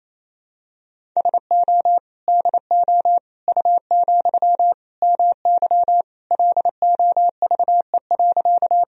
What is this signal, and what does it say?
Morse code, heard as a keyed tone, about 28 wpm, 700 Hz. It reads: SO DO U, MY LOVE.
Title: Sentences from Top 400 Words and Abbreviations 28wpm